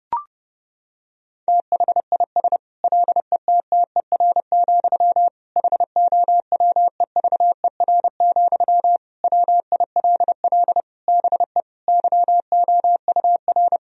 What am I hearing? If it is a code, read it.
Morse code, heard as a keyed tone, about 30 wpm, 700 Hz. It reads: THIS LETTER, HOWEVER, WILL BE YOUR
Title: Sentences from Top 1000 Words - Review Part 4 - 30wpm